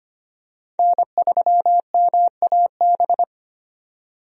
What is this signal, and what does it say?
Morse code, heard as a keyed tone, about 25 wpm, 700 Hz. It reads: N3MAB
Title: USA Call Signs - 5 Characters - 25wpm